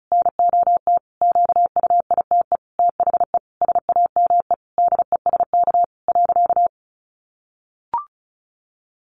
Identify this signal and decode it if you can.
Morse code, heard as a keyed tone, about 34 wpm, 700 Hz. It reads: NOT QUITE THE SAME DESK .
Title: A Sound Of Thunder 35wpm